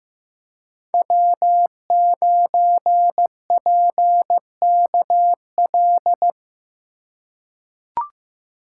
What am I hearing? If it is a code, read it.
Morse code, heard as a keyed tone, about 15 wpm, 700 Hz. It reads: W9PKL
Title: USA Call Signs - 5 Characters - 3x Character Spacing - 15wpm